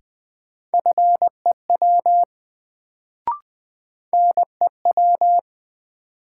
Morse code, heard as a keyed tone, about 20 wpm, 700 Hz. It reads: FEW NEW
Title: Sentences from Top 300 Words - New Word Drill - 20wpm